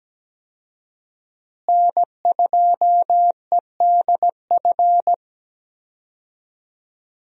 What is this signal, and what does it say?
Morse code, heard as a keyed tone, about 17 wpm, 700 Hz. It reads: N2EDF
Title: USA Call Signs - 5 Characters - Repeated 3x - 17wpm